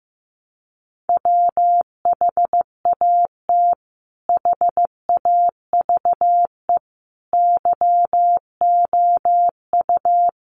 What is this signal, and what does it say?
Morse code, heard as a keyed tone, about 14 wpm, 700 Hz. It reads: WHAT HAVE YOU
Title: Sentences from Top 200 Words 15wpm